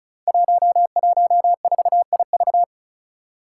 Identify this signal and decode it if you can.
Morse code, heard as a keyed tone, about 36 wpm, 700 Hz. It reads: 114IV
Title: Five Challenging Letter-Number Combinations 35wpm